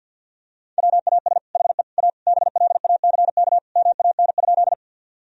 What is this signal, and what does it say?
Morse code, heard as a keyed tone, about 50 wpm, 700 Hz. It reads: WAS HE A BLACK MAN?
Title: Sentences from Top 1000 Words - Review Part 2 - 50wpm